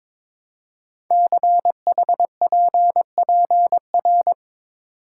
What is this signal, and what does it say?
Morse code, heard as a keyed tone, about 22 wpm, 700 Hz. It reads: CHPPR
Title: Code Groups - 5 Letters - 22wpm